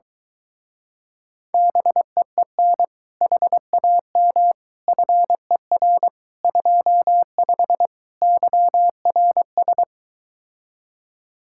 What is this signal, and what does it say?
Morse code, heard as a keyed tone, about 23 wpm, 700 Hz. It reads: BEEN HAM FER 25 YRS